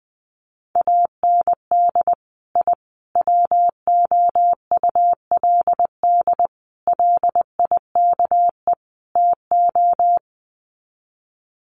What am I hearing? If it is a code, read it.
Morse code, heard as a keyed tone, about 20 wpm, 700 Hz. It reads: AND I WOULD LIKE TO
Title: Sets of 5 Words Encore 20wpm